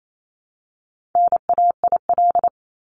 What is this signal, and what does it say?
Morse code, heard as a keyed tone, about 28 wpm, 700 Hz. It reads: NAIL
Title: Top 1501-1750 Words 28wpm